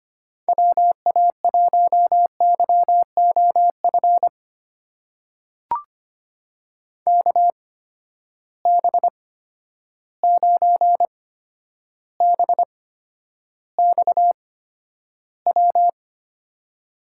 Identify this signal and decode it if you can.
Morse code, heard as a keyed tone, about 25 wpm, spaced as for 11 wpm, 700 Hz. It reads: WA1YOF KB9BXW
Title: USA Call Signs - 6 Characters - 8x Character Spacing - 25wpm